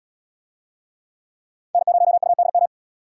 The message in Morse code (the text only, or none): N0UDM